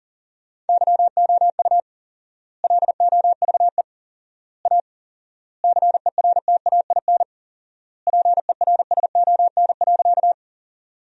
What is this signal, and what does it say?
Morse code, heard as a keyed tone, about 40 wpm, 700 Hz. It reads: YOU LOVE A CERTAIN PERSON.
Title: 4x Word-Spacing - Sentences from Top 500 Words - 40wpm